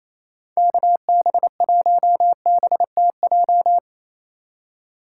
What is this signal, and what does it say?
Morse code, heard as a keyed tone, about 28 wpm, 700 Hz. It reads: KB1BTJ
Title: USA Call Signs - 6 Characters - 4x Character Spacing - 28wpm